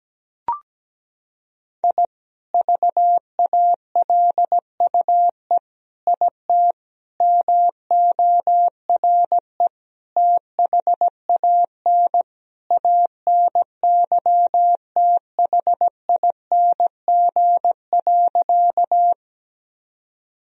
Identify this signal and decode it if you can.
Morse code, heard as a keyed tone, about 17 wpm, 700 Hz. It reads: I VALUE IT MORE THAN ANYTHING.
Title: Sentences from Top 1000 Words - Review Part 3 - 17wpm